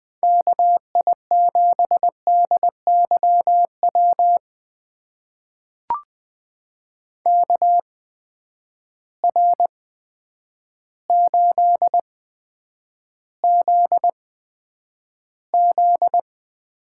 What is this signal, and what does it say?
Morse code, heard as a keyed tone, about 20 wpm, spaced as for 9 wpm, 700 Hz. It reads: KI7DYW KR8ZZ
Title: USA Call Signs - 8x Character Spacing - 20wpm